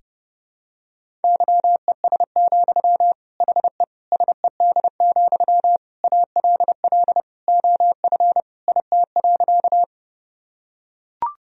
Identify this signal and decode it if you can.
Morse code, heard as a keyed tone, about 32 wpm, 700 Hz. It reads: YES, HE SED, ALL OF IT.
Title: Sentences from Top 500 Words and Abbreviations 30wpm